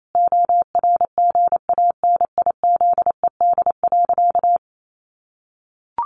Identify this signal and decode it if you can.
Morse code, heard as a keyed tone, about 28 wpm, 700 Hz. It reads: ORGANIZED.
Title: Sentences from Top 1300 Words 28wpm